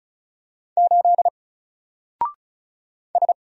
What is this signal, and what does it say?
Morse code, heard as a keyed tone, about 34 wpm, 700 Hz. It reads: 8 S
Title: Single Letter-Number Alt 35wpm